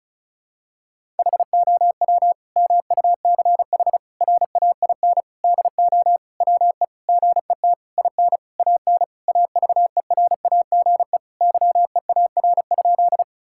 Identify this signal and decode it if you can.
Morse code, heard as a keyed tone, about 36 wpm, 700 Hz. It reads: HOW MUCH RAIN DO WE GET IN AN AVERAGE YEAR?
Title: Sentences from Top 1000 Words - Review Part 6 - 35wpm